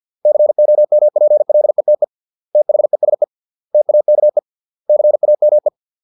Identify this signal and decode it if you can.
Morse code, heard as a keyed tone, about 50 wpm, 600 Hz. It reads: COMPLETE THESE TAKE CAME